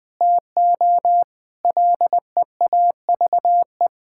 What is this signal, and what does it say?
Morse code, heard as a keyed tone, about 19 wpm, 700 Hz. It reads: TO LEAVE
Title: Sentences with 3-Word Phrases - Spoken Beginning - 20wpm